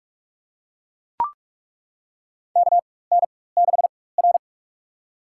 Morse code, heard as a keyed tone, about 42 wpm, spaced as for 21 wpm, 700 Hz. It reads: KN6R
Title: USA Call Signs - 4 Characters - 4x Character Spacing - 45wpm